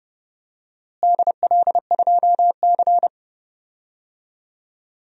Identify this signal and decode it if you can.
Morse code, heard as a keyed tone, about 29 wpm, 700 Hz. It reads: DL2C
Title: DXCC-100 Call Signs - 4 Characters - Repeated 2x - 30wpm